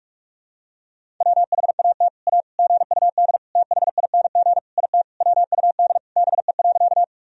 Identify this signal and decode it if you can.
Morse code, heard as a keyed tone, about 41 wpm, 700 Hz. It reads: WHAT A GUD THING IT WUD BE.
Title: Sentences from Top 200 Words and Abbreviations 45wpm